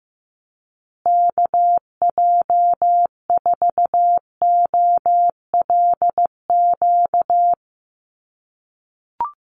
Morse code, heard as a keyed tone, about 14 wpm, 700 Hz. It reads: KJ4OLQ
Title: USA Call Signs Repeated 2x 15wpm